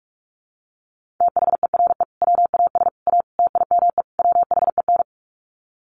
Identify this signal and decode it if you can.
Morse code, heard as a keyed tone, about 44 wpm, 700 Hz. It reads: THERE WAS A TIME WHEN
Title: Sets of 5 Words 45wpm